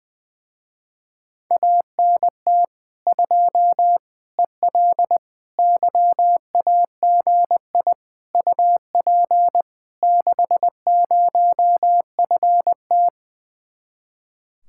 Morse code, heard as a keyed tone, about 20 wpm, 700 Hz. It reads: ANT 2 EL YAGI UP 60FT
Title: Speed-Racing Common QSO Phrases - 20wpm